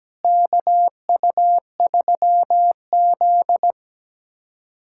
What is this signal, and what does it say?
Morse code, heard as a keyed tone, about 17 wpm, 700 Hz. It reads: KU3Z
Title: USA Callsign DE Callsign 17wpm